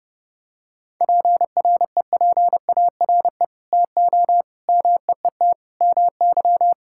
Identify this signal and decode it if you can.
Morse code, heard as a keyed tone, about 30 wpm, 700 Hz. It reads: PREPARE TO MEET MY
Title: Sentences from Top 1000 Words - Review Part 6 - 30wpm